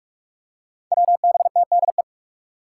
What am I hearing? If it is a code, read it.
Morse code, heard as a keyed tone, about 46 wpm, 700 Hz. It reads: WBTDE
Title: Five Challenging Letter-Number Combinations 45wpm